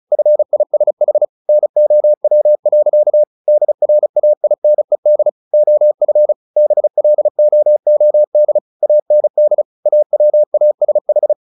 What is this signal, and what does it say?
Morse code, heard as a keyed tone, about 34 wpm, 600 Hz. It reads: FISH NOW. DRAINED OF BLOOD AND AWASH
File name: the old man and the sea 2 35wpm